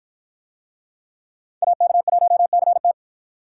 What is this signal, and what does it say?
Morse code, heard as a keyed tone, about 53 wpm, 700 Hz. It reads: AK1CT